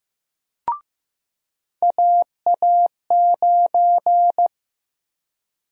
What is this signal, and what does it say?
Morse code, heard as a keyed tone, about 15 wpm, 700 Hz. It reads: AA9
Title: USA Call Sign Prefixes - Repeated 2x - 15wpm